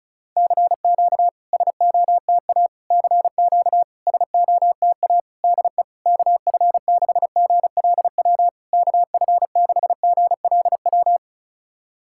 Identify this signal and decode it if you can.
Morse code, heard as a keyed tone, about 36 wpm, 700 Hz. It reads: CQ SOTA CQ SOTA DE KF6GLW KF6GLW
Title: SOTA QSO Practice 35wpm